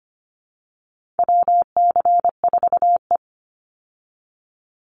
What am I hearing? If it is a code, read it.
Morse code, heard as a keyed tone, about 25 wpm, 700 Hz. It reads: WC4E
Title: Call Signs Repeated 2x - 25wpm